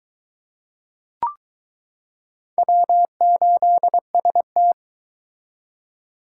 Morse code, heard as a keyed tone, about 23 wpm, 700 Hz. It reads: W8ST